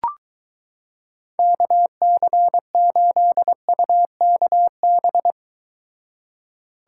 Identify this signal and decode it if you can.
Morse code, heard as a keyed tone, about 23 wpm, 700 Hz. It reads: KC8UKB